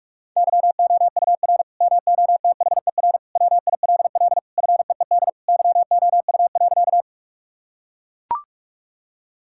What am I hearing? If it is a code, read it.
Morse code, heard as a keyed tone, about 43 wpm, 700 Hz. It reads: YOUR MOTHER WILL FEED YOU.
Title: Sentences from Top 1600 Words 45wpm